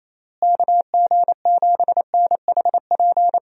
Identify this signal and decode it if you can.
Morse code, heard as a keyed tone, about 28 wpm, 700 Hz. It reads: KG7NHP
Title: USA Call Signs - 6 Characters - Repeated 3x - 28wpm